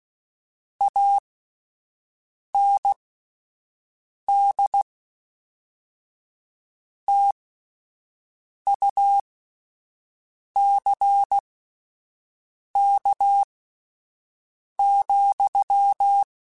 Morse code, heard as a keyed tone, about 16 wpm, spaced as for 5 wpm, 800 Hz. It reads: AND TUCK,
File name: Misc / Misc-mp3s / Word Pairs @5